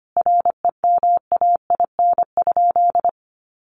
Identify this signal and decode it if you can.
Morse code, heard as a keyed tone, about 25 wpm, 700 Hz. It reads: REMAIN?
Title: Sentences from Top 1200 Words 25wpm